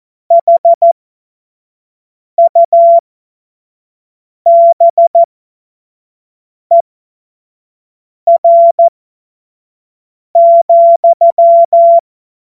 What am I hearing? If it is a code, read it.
Morse code, heard as a keyed tone, about 14 wpm, spaced as for 5 wpm, 680 Hz. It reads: HUBER,
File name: Morsekurs / Witze / Witz 14. März 2023 25 F70